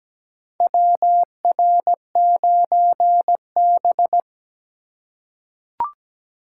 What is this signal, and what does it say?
Morse code, heard as a keyed tone, about 17 wpm, 700 Hz. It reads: WR9B
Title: USA Call Signs - 2x Character Spacing - 17wpm